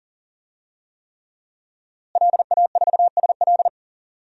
Morse code, heard as a keyed tone, about 37 wpm, 700 Hz. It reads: LA4SL
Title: DXCC-100 Call Signs - 5 Characters - Repeated 3x - 40wpm